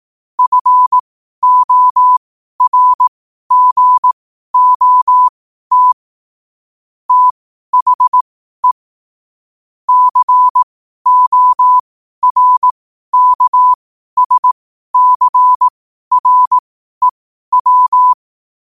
Morse code, heard as a keyed tone, about 18 wpm, spaced as for 12 wpm, 1000 Hz. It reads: FORGOT THE CORKSCREW
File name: qotdfiles / 12wpm